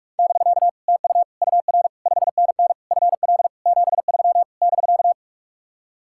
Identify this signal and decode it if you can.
Morse code, heard as a keyed tone, about 49 wpm, 700 Hz. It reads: <BK> TU UR 5NN FL 73 <BK>